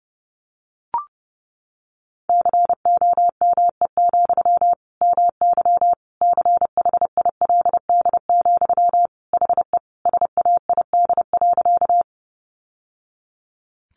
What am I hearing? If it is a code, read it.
Morse code, heard as a keyed tone, about 30 wpm, 700 Hz. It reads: COME, MY CHILD, HE SAID.
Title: Sentences from Top 800 Words 30wpm